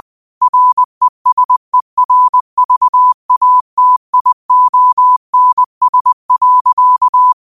Morse code, heard as a keyed tone, about 20 wpm, 1000 Hz. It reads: RESERVATIONS.
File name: qotdfiles / 20wpm